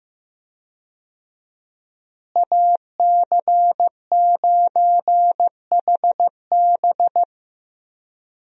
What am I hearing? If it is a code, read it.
Morse code, heard as a keyed tone, about 15 wpm, 700 Hz. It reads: AC9HB
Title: USA Call Signs - 5 Characters - Repeated 3x - 15wpm